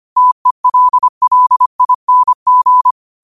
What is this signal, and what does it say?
Morse code, heard as a keyed tone, about 25 wpm, 1000 Hz. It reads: TELLING
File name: qotdfiles / 25wpm